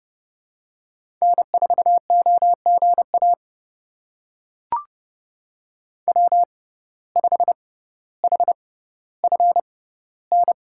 Morse code, heard as a keyed tone, about 30 wpm, spaced as for 15 wpm, 700 Hz. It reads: N4OGA W5HFN
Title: USA Call Signs - 5 Characters - 6x Character Spacing - 30wpm